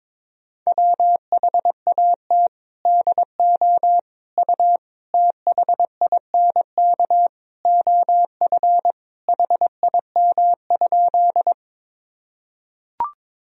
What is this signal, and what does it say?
Morse code, heard as a keyed tone, about 22 wpm, 700 Hz. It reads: WHAT DO U THINK OF HIM?
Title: Sentences from Top 200 Words and Abbreviations 22wpm